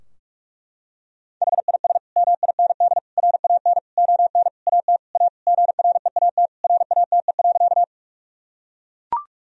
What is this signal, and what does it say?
Morse code, heard as a keyed tone, about 41 wpm, 700 Hz. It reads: HIS MIND RAN ON AT A GREAT RATE.
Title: Sentences from Top 1000 Words - Review Part 4 - 45wpm